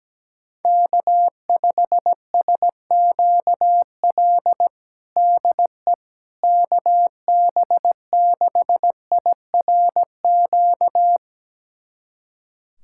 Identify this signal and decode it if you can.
Morse code, heard as a keyed tone, about 17 wpm, 700 Hz. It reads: K5SQL DE KB6IRQ
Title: USA Callsign DE Callsign 17wpm